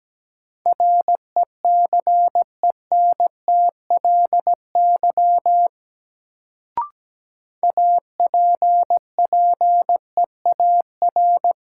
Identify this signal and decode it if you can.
Morse code, heard as a keyed tone, about 17 wpm, 700 Hz. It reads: RECENTLY APPEAR
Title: Sentences from Top 1200 Words - New Word Drill Alt - 17wpm